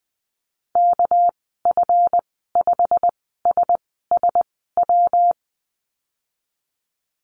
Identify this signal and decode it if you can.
Morse code, heard as a keyed tone, about 20 wpm, spaced as for 15 wpm, 700 Hz. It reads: KF5SSW
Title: 207 Lesson 33g - Letter j - Call Signs 20wpm